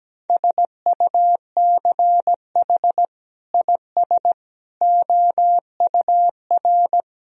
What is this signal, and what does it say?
Morse code, heard as a keyed tone, about 17 wpm, 700 Hz. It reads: SUCH IS OUR
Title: Sentences from Top 1000 Words - Review Part 2 - 17wpm